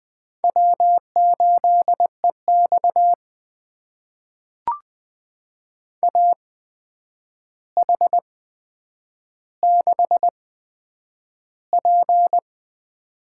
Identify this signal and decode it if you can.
Morse code, heard as a keyed tone, about 20 wpm, spaced as for 9 wpm, 700 Hz. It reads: W8EX AH6P
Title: USA Call Signs - 4 Characters - 8x Character Spacing - 20wpm